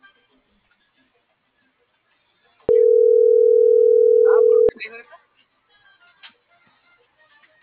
Ringback tone (North America), 440 + 480 Hz.